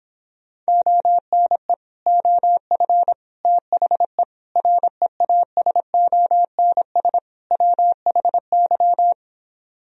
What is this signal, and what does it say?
Morse code, heard as a keyed tone, about 26 wpm, 700 Hz. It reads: ONE OF THE REASONS WHY